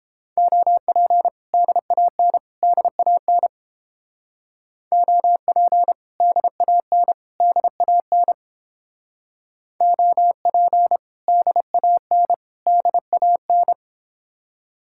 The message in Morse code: OP DAN DAN OP DAN DAN OP DAN DAN

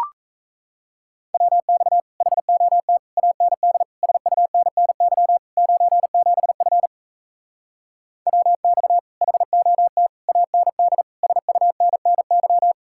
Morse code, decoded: WX HOT AND SUNNY 97F WX HOT AND SUNNY

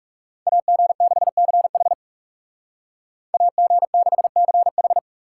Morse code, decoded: AG6CH AG6CH

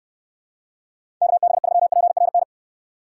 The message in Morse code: KB3PRN

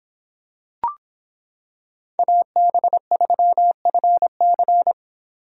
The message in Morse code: AB3FC

